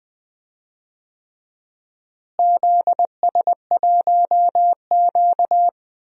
Morse code, heard as a keyed tone, about 21 wpm, 700 Hz. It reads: ZS1Q
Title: DXCC-100 Call Signs - 4 Characters - Repeated 3x - 20wpm